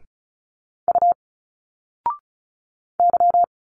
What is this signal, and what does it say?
Morse code, heard as a keyed tone, about 36 wpm, 700 Hz. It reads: U Y
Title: Single Letters 35wpm